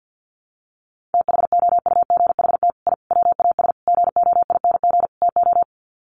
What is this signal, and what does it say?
Morse code, heard as a keyed tone, about 51 wpm, 700 Hz. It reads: THOUGHT I WAS GOING TO
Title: Sets of 5 Words Encore2 50wpm